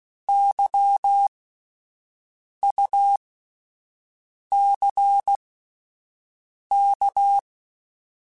Morse code, heard as a keyed tone, about 16 wpm, spaced as for 5 wpm, 800 Hz. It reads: YUCK
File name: Alphabetical / Alphabetical mp3s / Ys @5